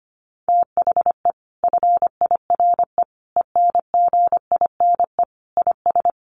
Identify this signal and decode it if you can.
Morse code, heard as a keyed tone, about 25 wpm, 700 Hz. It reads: THE FIRE ENGINE IS
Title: Sentences from Top 1500 Words 25wpm